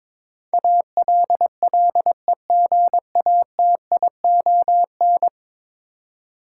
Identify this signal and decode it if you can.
Morse code, heard as a keyed tone, about 22 wpm, 700 Hz. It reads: ALLEGATION
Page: morse-code-ninja-practice-22wpm